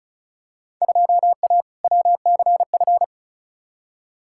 Morse code, heard as a keyed tone, about 36 wpm, 690 Hz. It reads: 2A WCF
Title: Field Day Exchanges Encore - 35wpm